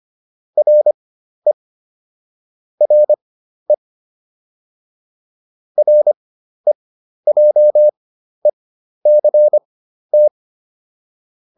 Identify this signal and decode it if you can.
Morse code, heard as a keyed tone, about 25 wpm, spaced as for 12 wpm, 600 Hz. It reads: RE RE REJECT